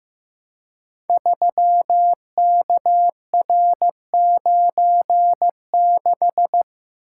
Morse code, heard as a keyed tone, about 15 wpm, 700 Hz. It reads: 3KR96
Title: Code Groups - 5 Letter:Numbers - 15wpm